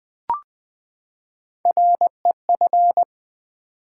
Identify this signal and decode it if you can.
Morse code, heard as a keyed tone, about 20 wpm, 700 Hz. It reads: REF
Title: Build A Word - 3 Letters At A Time - 6-Letter Words - 20wpm